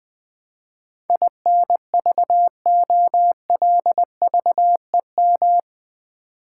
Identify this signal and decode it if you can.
Morse code, heard as a keyed tone, about 20 wpm, 700 Hz. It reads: INVOLVEM
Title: Build A Word - 4 Letters At A Time - 11-Letter Words - 20wpm